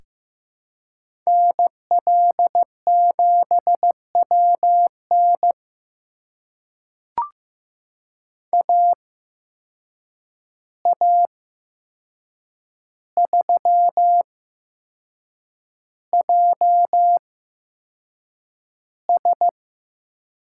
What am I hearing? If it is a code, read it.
Morse code, heard as a keyed tone, about 15 wpm, spaced as for 6 wpm, 700 Hz. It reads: NL7WN AA3JS
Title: USA Call Signs - 5 Characters - 8x Character Spacing - 15wpm